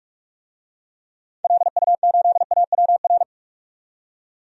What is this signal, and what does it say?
Morse code, heard as a keyed tone, about 41 wpm, 700 Hz. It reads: LU8AWR
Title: DXCC-100 Call Signs - 6 Characters - 45wpm